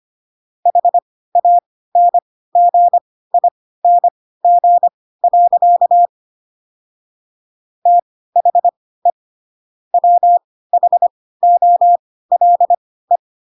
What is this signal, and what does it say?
Morse code, heard as a keyed tone, about 25 wpm, spaced as for 16 wpm, 700 Hz. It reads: HANGING. THE WHOLE
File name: Pinocchio / Pinocchio ch6 10 25 ik1hge